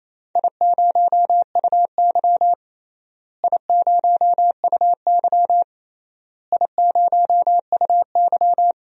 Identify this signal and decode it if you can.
Morse code, heard as a keyed tone, about 28 wpm, 700 Hz. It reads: I0UY I0UY I0UY
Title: DXCC-100 Call Signs - 4 Characters - Repeated 3x - 28wpm